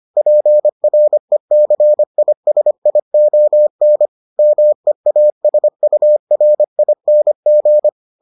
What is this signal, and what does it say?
Morse code, heard as a keyed tone, about 25 wpm, 600 Hz. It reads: PRECISION MEASURING